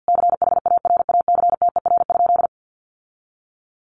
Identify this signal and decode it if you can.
Morse code, heard as a keyed tone, about 50 wpm, 700 Hz. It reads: CHARACTER?
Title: Sentences from Top 1200 Words 50wpm